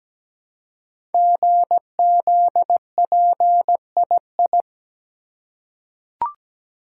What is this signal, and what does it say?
Morse code, heard as a keyed tone, about 17 wpm, 700 Hz. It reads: GZPII
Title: Code Groups - 5 Letter:Numbers - 17wpm